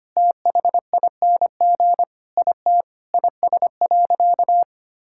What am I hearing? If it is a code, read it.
Morse code, heard as a keyed tone, about 25 wpm, 700 Hz. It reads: THING IT IS.
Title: Sentences from Top 1000 Words - Review Part 6 - 25wpm